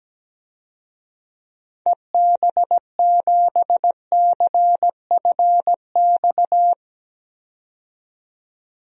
Morse code, heard as a keyed tone, about 17 wpm, 700 Hz. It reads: EB7CFX
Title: DXCC-100 Call Signs - 6 Characters - 17wpm